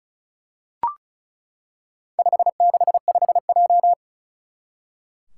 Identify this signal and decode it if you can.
Morse code, heard as a keyed tone, about 35 wpm, 700 Hz. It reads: 565J
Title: Four Challenging Letter-Number Combinations 35wpm